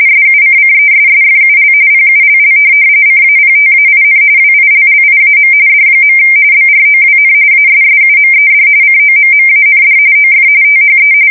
RTTY, about 45.45 baud. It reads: SUCCESSFUL PULL CAPTAIN WILLIAM D. BROWN ASSUMED COMMAND OF THE
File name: fsk170 / USS-Missouri